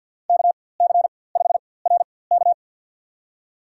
Morse code, heard as a keyed tone, about 52 wpm, spaced as for 23 wpm, 700 Hz. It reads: KC5RK
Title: USA Call Signs - 4x Character Spacing - 50wpm